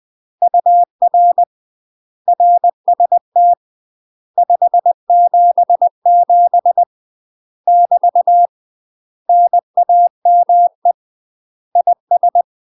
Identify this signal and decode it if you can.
Morse code, heard as a keyed tone, about 20 wpm, 700 Hz. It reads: UR RST 577 <BT> NAME IS
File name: UK MASTER QSOs TRAINING/20WPM